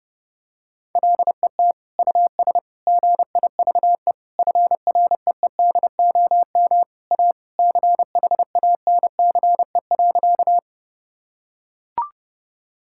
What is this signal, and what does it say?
Morse code, heard as a keyed tone, about 30 wpm, 700 Hz. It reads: LET US GIVE FREEDOM A CHANCE.
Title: Sentences from Top 1000 Words - Review Part 5 - 30wpm